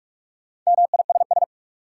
Morse code, heard as a keyed tone, about 47 wpm, 700 Hz. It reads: MISS
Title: Sentences from Top 500 Words - New Word Drill - 45wpm